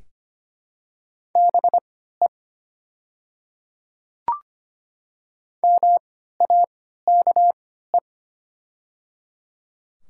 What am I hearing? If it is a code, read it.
Morse code, heard as a keyed tone, about 25 wpm, spaced as for 14 wpm, 700 Hz. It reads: BE MAKE
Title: 3x Character Spacing - Top 300 Words - 25wpm